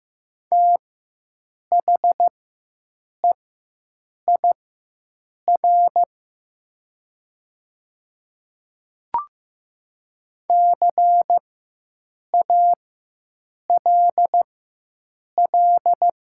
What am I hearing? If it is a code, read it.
Morse code, heard as a keyed tone, about 15 wpm, spaced as for 7 wpm, 700 Hz. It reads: THEIR CALL
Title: 4x Character Spacing - Top 300 Words - 15wpm